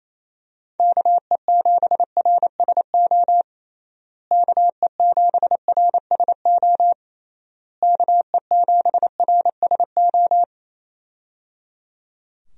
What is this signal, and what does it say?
Morse code, heard as a keyed tone, about 28 wpm, 700 Hz. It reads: KE7RSO KE7RSO KE7RSO
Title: USA Call Signs - 6 Characters - Repeated 3x - 28wpm